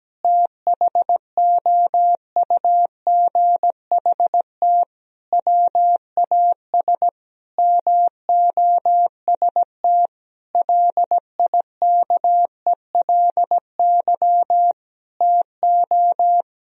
Morse code, heard as a keyed tone, about 17 wpm, 700 Hz. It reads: THOUGHT WAS MOST LIKELY TO
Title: Sentences from Top 1000 Words - Review Part 4 - 17wpm